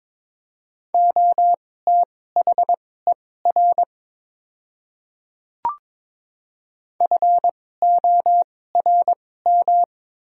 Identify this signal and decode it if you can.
Morse code, heard as a keyed tone, about 22 wpm, spaced as for 16 wpm, 700 Hz. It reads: OTHER FORM
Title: 2x Character Spacing - Top 200 Words - 22wpm